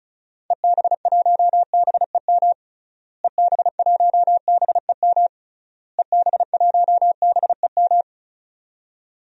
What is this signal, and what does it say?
Morse code, heard as a keyed tone, about 34 wpm, 700 Hz. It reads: EB1BEM EB1BEM EB1BEM
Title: DXCC-100 Call Signs - 6 Characters - Repeated 3x - 35wpm